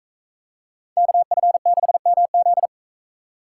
Morse code, heard as a keyed tone, about 38 wpm, 700 Hz. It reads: KF6MZ